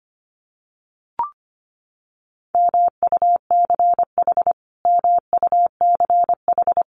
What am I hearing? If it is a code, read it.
Morse code, heard as a keyed tone, about 25 wpm, 700 Hz. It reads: MUCH MUCH
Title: Top 200 Words Repeated 25wpm